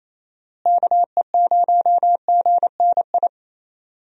Morse code, heard as a keyed tone, about 27 wpm, 700 Hz. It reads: KE0GNI
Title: USA Call Signs - 6 Characters - Repeated 2x - 28wpm